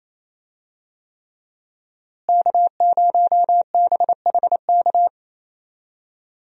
Morse code, heard as a keyed tone, about 28 wpm, 700 Hz. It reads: K0BHK
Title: USA Call Signs - 5 Characters - Repeated 3x - 28wpm